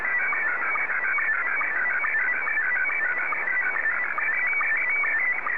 SSTV (slow-scan television), Scottie 1.